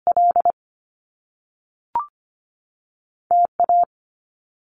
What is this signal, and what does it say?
Morse code, heard as a keyed tone, about 25 wpm, 700 Hz. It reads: L T A
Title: Top Two Letter Combinations in Words 25wpm